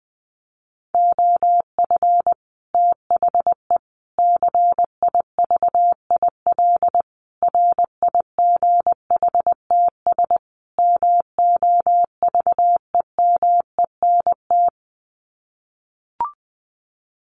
Morse code, heard as a keyed tone, about 20 wpm, 700 Hz. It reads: OF THE CIVIL RIGHTS MOVEMENT
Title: Sets of 5 Words Encore2 20wpm